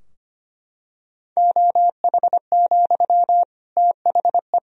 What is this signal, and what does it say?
Morse code, heard as a keyed tone, about 25 wpm, 700 Hz. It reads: OH, THE
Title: Sentences from Top 1000 Words - Review Part 3 - 25wpm